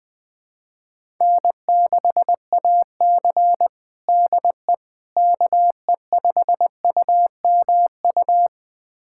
Morse code, heard as a keyed tone, about 19 wpm, 700 Hz. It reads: N6AC DE KE5UMU
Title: USA Callsign DE Callsign 20wpm